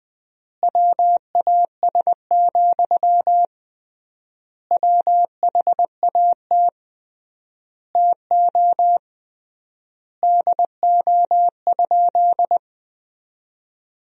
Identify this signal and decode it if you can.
Morse code, heard as a keyed tone, about 21 wpm, 700 Hz. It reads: WAS, WHAT TO DO?
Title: 3x Word-Spacing - Sentences from Top 400 Words - 20wpm